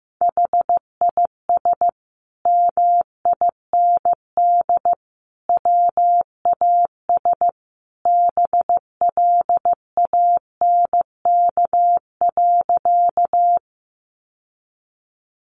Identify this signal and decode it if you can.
Morse code, heard as a keyed tone, about 14 wpm, 700 Hz. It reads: HIS MIND WAS BLANK.
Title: Sentences from Top 1500 Words 15wpm